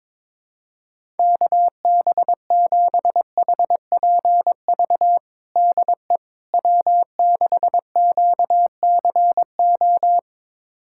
Morse code, heard as a keyed tone, about 22 wpm, 700 Hz. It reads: KB7HPV DE W6QCO
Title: USA Callsign DE Callsign 22wpm